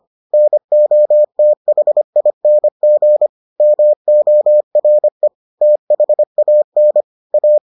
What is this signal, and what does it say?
Morse code, heard as a keyed tone, about 25 wpm, 600 Hz. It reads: NOTHING MORE THAN A